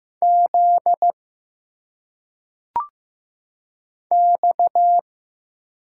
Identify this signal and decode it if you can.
Morse code, heard as a keyed tone, about 15 wpm, 700 Hz. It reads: Z X
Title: Single Letters - Phonetic Pronunciation 15wpm